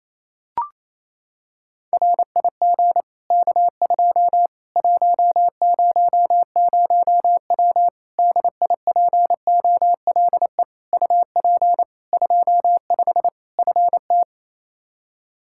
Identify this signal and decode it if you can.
Morse code, heard as a keyed tone, about 28 wpm, 700 Hz. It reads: RIG K2 100W DIPOLE UP 25 FT
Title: Practice QSO - Exchange 2 - Info - 28wpm